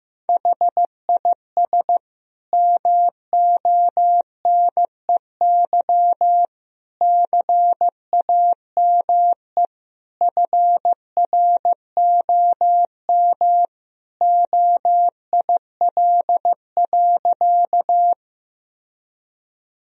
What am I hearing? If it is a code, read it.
Morse code, heard as a keyed tone, about 15 wpm, 700 Hz. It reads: HIS MONEY CAME FROM OIL.
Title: Sentences from Top 1000 Words - Review Part 4 - 15wpm